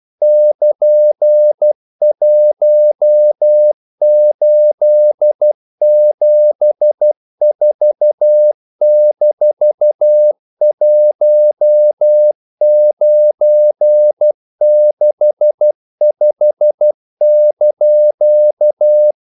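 Morse code, heard as a keyed tone, about 12 wpm, 600 Hz. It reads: <KN>1874-1965)
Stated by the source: Morse practice sending English quotations.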